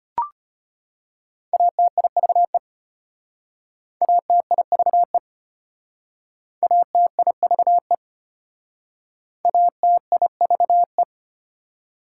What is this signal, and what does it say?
Morse code, heard as a keyed tone, about 31 wpm, 690 Hz. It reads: ATIVE ATIVE ATIVE ATIVE
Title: Speed-Racing Prefixes and Suffixes 25wpm